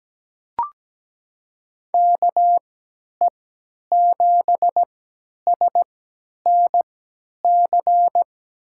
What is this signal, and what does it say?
Morse code, heard as a keyed tone, about 17 wpm, spaced as for 10 wpm, 700 Hz. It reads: KE7SNC
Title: USA Call Signs - 6 Characters - 3x Character Spacing - 17wpm